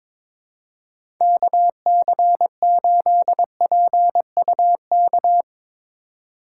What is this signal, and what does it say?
Morse code, heard as a keyed tone, about 22 wpm, 700 Hz. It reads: KC8PUK
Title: USA Call Signs - 6x Character Spacing - 22wpm